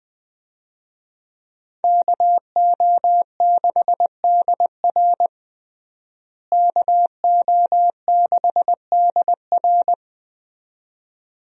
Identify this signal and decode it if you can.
Morse code, heard as a keyed tone, about 21 wpm, 700 Hz. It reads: KO6DR KO6DR
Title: USA Call Signs - 5 Characters - Repeated 2x - 20wpm